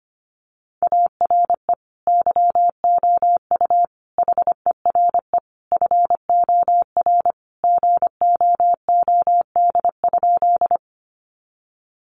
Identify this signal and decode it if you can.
Morse code, heard as a keyed tone, about 25 wpm, 700 Hz. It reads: ARE YOU HERE FOR GOOD?
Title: Sentences from Top 200 Words 25wpm